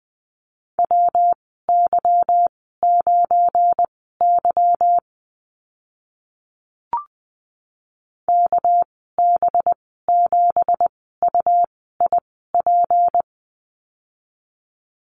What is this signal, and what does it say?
Morse code, heard as a keyed tone, about 20 wpm, spaced as for 15 wpm, 700 Hz. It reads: WY9Y KB7UIP
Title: 207 Lesson 33g - Letter j - Call Signs 20wpm